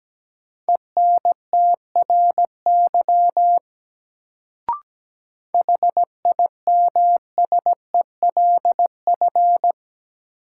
Morse code, heard as a keyed tone, about 17 wpm, 700 Hz. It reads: ENTRY HIMSELF
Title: Sentences from Top 1500 Words - New Word Drill Alt - 17wpm